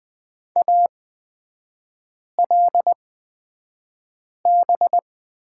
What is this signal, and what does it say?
Morse code, heard as a keyed tone, about 20 wpm, spaced as for 5 wpm, 700 Hz. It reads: ALB
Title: 236 Lesson 38e - Letter q - Three Characters 20wpm